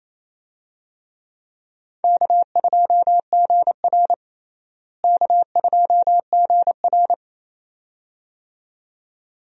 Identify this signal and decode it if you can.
Morse code, heard as a keyed tone, about 28 wpm, 700 Hz. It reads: K2GR K2GR
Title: USA Call Signs - 4 Characters - Repeated 2x - 28wpm